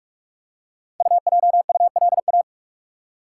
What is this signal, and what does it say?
Morse code, heard as a keyed tone, about 44 wpm, 700 Hz. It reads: UJULA
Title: Code Groups - 5 Letters - 45wpm